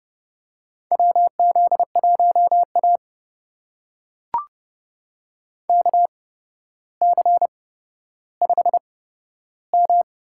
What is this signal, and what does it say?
Morse code, heard as a keyed tone, about 32 wpm, spaced as for 13 wpm, 700 Hz. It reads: WZ1A KC5M
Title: USA Call Signs - 4 Characters - 8x Character Spacing - 30wpm